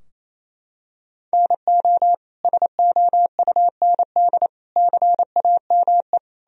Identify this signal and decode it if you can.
Morse code, heard as a keyed tone, about 28 wpm, 700 Hz. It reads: NO SOUND CAME
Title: Sentences from Top 1000 Words - Review Part 4 - 28wpm